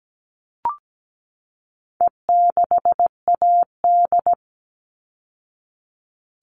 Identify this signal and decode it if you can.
Morse code, heard as a keyed tone, about 17 wpm, 700 Hz. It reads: E6AD
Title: All Cut Numbers - 4-Digit 17wpm